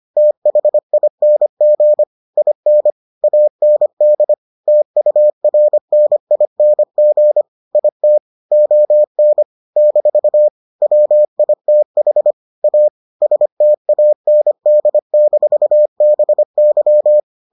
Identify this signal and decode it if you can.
Morse code, heard as a keyed tone, about 25 wpm, 600 Hz. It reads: THING IN AND TURNING IT ON - WITH A STAND-BY